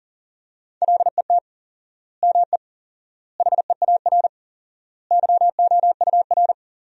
Morse code, heard as a keyed tone, about 40 wpm, 700 Hz. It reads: LET ME HEAR YOUR
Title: 4x Word-Spacing - Sentences from Top 300 Words - 40wpm